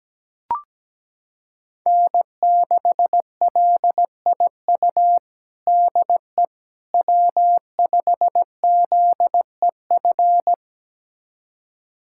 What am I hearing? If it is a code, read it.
Morse code, heard as a keyed tone, about 17 wpm, 700 Hz. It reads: N6LIU DE W5ZEF
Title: USA Callsign DE Callsign - 5 Characters - 17wpm